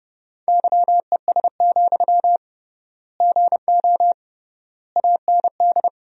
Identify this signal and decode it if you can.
Morse code, heard as a keyed tone, about 30 wpm, 700 Hz. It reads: YES, GO AND
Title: 3x Word-Spacing - Sentences from Top 500 Words - 30wpm